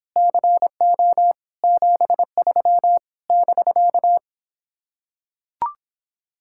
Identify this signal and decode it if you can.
Morse code, heard as a keyed tone, about 26 wpm, 700 Hz. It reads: CO 73 <BK>